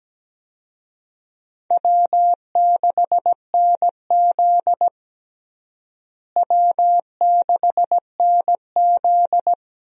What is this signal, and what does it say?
Morse code, heard as a keyed tone, about 17 wpm, 700 Hz. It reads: W6NZ W6NZ
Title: USA Call Signs - 4 Characters - Repeated 2x - 17wpm